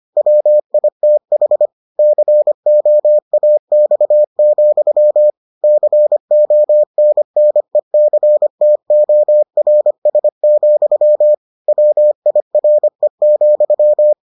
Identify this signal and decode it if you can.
Morse code, heard as a keyed tone, about 25 wpm, 600 Hz. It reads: WITH COAX, CONNECTORS, WIRE,